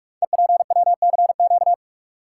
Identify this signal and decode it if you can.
Morse code, heard as a keyed tone, about 44 wpm, 700 Hz. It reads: EPWCQ
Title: Code Groups - 5 Letter:Numbers - 45wpm